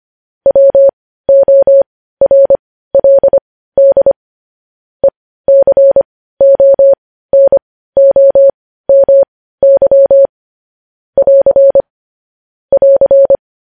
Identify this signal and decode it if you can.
Morse code, heard as a keyed tone, about 25 wpm, spaced as for 15 wpm, 550 Hz. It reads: WORLD ECONOMY <AR> <AR>
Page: news-headlines-in-morse-code-at-15-wpm